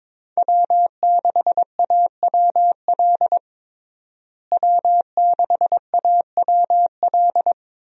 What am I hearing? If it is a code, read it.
Morse code, heard as a keyed tone, about 22 wpm, 700 Hz. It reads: W6AWL W6AWL
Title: USA Call Signs - 5 Characters - Repeated 2x - 22wpm